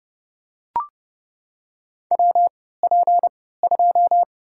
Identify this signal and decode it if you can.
Morse code, heard as a keyed tone, about 29 wpm, spaced as for 17 wpm, 700 Hz. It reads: WP2
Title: USA Call Signs - 3x Character Spacing - 30wpm